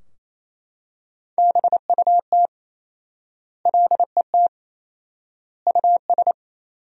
Morse code, heard as a keyed tone, about 28 wpm, 700 Hz. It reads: BUT LET US
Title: 4x Word-Spacing - Sentences from Top 500 Words - 28wpm